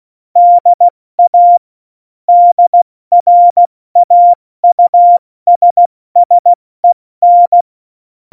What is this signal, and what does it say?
Morse code, heard as a keyed tone, about 16 wpm, spaced as for 14 wpm, 700 Hz. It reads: DA DRAUSSEN
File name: Morsekurs / Witze / Witz 23. Oktober 2018, 70 F80